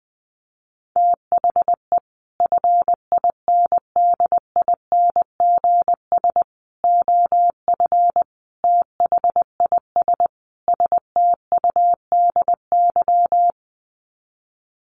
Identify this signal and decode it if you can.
Morse code, heard as a keyed tone, about 20 wpm, 700 Hz. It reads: THE FINDINGS OF THIS STUDY
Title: Sets of 5 Words Encore2 20wpm